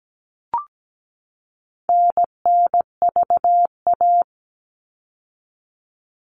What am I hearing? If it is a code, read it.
Morse code, heard as a keyed tone, about 17 wpm, 700 Hz. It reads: NNVA
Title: All Cut Numbers - 4-Digit 17wpm